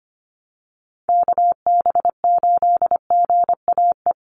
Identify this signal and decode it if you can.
Morse code, heard as a keyed tone, about 25 wpm, 700 Hz. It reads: KB8GAE
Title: Call Signs Repeated 2x - 25wpm